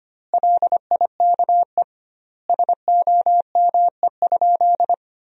Morse code, heard as a keyed tone, about 25 wpm, 700 Hz. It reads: LIKE SOME?
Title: 2x Word-Spacing - Sentences from Top 100 Words - 25wpm